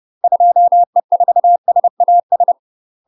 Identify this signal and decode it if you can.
Morse code, heard as a keyed tone, about 30 wpm, 700 Hz. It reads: 2E4SAS
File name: UK MASTER QSOs TRAINING / 30WPM / UK MASTER QSOs 0011